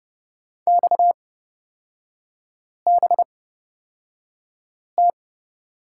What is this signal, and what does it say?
Morse code, heard as a keyed tone, about 30 wpm, spaced as for 5 wpm, 700 Hz. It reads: XBT
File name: Lesson 37e - Letter x - Three Characters 30wpm